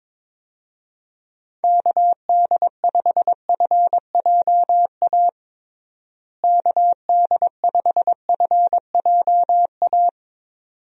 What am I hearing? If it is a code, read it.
Morse code, heard as a keyed tone, about 22 wpm, 700 Hz. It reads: KD5FJA KD5FJA
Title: USA Call Signs - 6 Characters - Repeated 2x - 22wpm